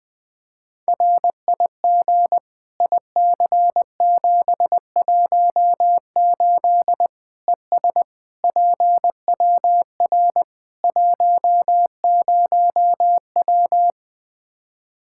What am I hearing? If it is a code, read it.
Morse code, heard as a keyed tone, about 20 wpm, 700 Hz. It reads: RIG IC718 ES PWR 10W
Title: Practice QSO - Exchange 2 - Info - 20wpm